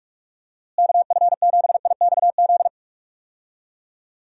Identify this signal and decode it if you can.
Morse code, heard as a keyed tone, about 48 wpm, 700 Hz. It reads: KF7IXZ
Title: USA Call Signs - 6 Characters - Repeated 2x - 45wpm